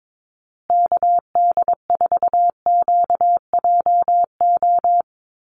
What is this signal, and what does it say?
Morse code, heard as a keyed tone, about 22 wpm, 700 Hz. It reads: KD4QJO
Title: USA Call Signs Repeated 2x 22wpm